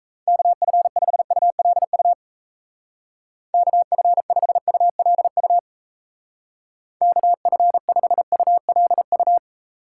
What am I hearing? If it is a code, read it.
Morse code, heard as a keyed tone, about 36 wpm, 700 Hz. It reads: KF5ULU KF5ULU KF5ULU